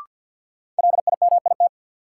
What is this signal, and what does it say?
Morse code, heard as a keyed tone, about 47 wpm, 700 Hz. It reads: LIMIT
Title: Sentences from Top 1500 Words - New Word Drill Alt - 50wpm